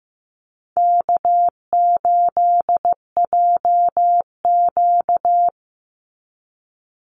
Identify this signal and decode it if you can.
Morse code, heard as a keyed tone, about 15 wpm, 700 Hz. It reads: K8JQ
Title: Easy Call Signs Repeated 3x - 15wpm